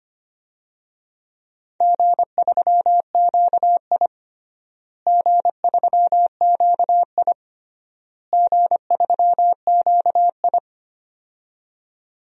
Morse code, heard as a keyed tone, about 25 wpm, 700 Hz. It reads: G3QI G3QI G3QI
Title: DXCC-100 Call Signs - 4 Characters - Repeated 3x - 25wpm